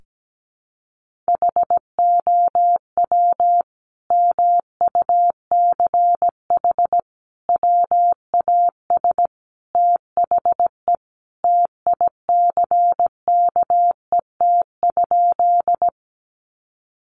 Morse code, heard as a keyed tone, about 17 wpm, 700 Hz. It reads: HOW MUCH WAS THE TICKET?
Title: Sentences from Top 1400 Words 17wpm